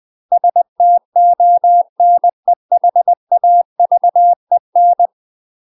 Morse code, heard as a keyed tone, about 20 wpm, 700 Hz. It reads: STONEHAVEN
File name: UK MASTER QSOs TRAINING / 20WPM / UK MASTER QSOs 0060